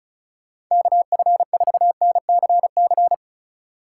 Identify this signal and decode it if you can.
Morse code, heard as a keyed tone, about 44 wpm, 700 Hz. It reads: KF4NCC